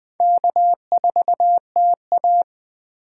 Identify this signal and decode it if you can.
Morse code, heard as a keyed tone, about 20 wpm, 700 Hz. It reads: K4TA